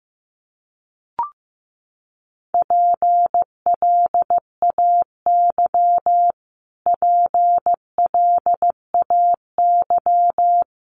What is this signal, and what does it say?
Morse code, heard as a keyed tone, about 15 wpm, 700 Hz. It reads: PLAY PLAY
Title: Top 500 Words Repeated 15wpm